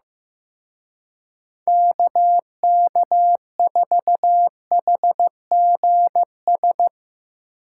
Morse code, heard as a keyed tone, about 17 wpm, 700 Hz. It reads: KK4HGS
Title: Speed Racing - USA 2x3 Call Signs 15wpm